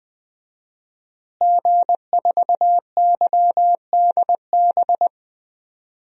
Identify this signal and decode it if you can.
Morse code, heard as a keyed tone, about 20 wpm, 700 Hz. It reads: G4YDB
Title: DXCC-100 Call Signs - 5 Characters - Repeated 2x - 20wpm